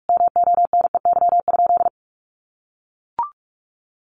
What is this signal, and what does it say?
Morse code, heard as a keyed tone, about 44 wpm, 700 Hz. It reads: MONEY?
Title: Sentences from Top 1100 Words 45wpm